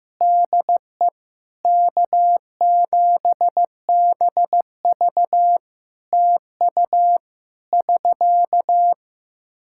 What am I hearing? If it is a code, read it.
Morse code, heard as a keyed tone, about 15 wpm, 700 Hz. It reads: DE K7BV TU <SK>
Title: Practice QSO - Exchange 4 - Ending - 15wpm